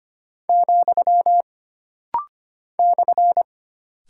Morse code, heard as a keyed tone, about 25 wpm, 700 Hz. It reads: , /
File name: Single Punctuation-Prosign - Rapid-Fire - 25wpm